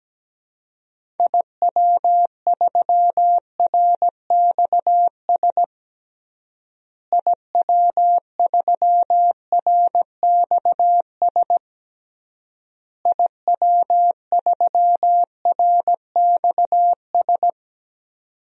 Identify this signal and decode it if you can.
Morse code, heard as a keyed tone, about 17 wpm, 700 Hz. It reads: IW3RXS IW3RXS IW3RXS
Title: DXCC-100 Call Signs - 6 Characters - Repeated 3x - 17wpm